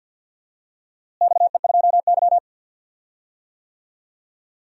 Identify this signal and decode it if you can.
Morse code, heard as a keyed tone, about 51 wpm, 700 Hz. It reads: XE2Y
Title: DXCC-100 Call Signs - 4 Characters - Repeated 2x - 50wpm